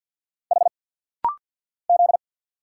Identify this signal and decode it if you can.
Morse code, heard as a keyed tone, about 48 wpm, 690 Hz. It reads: H Z
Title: Single Letters - Rapid-Fire 50wpm